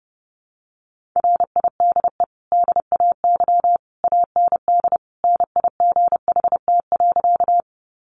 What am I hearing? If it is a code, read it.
Morse code, heard as a keyed tone, about 30 wpm, 700 Hz. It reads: RIDE DAY AND NIGHT.
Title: Sentences from Top 1200 Words 30wpm